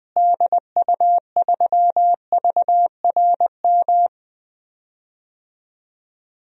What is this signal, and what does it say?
Morse code, heard as a keyed tone, about 21 wpm, 700 Hz. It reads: DU3VRM
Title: DXCC-100 Call Signs - 6 Characters - Repeated 3x - 20wpm